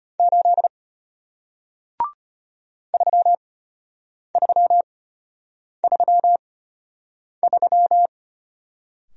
Morse code, heard as a keyed tone, about 33 wpm, spaced as for 8 wpm, 700 Hz. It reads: 8 3333